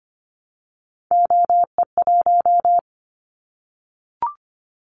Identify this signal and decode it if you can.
Morse code, heard as a keyed tone, about 25 wpm, 700 Hz. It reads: OE1
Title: Callsign Prefixes for Top 100 DXCC - 25wpm